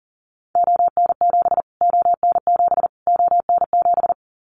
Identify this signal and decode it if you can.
Morse code, heard as a keyed tone, about 40 wpm, 700 Hz. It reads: ON7 ON7 ON7
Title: Callsign Prefixes for Top 25 DXCC - Repeated 3x - 40wpm